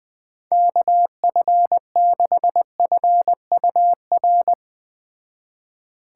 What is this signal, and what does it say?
Morse code, heard as a keyed tone, about 19 wpm, 700 Hz. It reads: KF6FUR
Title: Speed Racing - USA 2x3 Call Signs 17wpm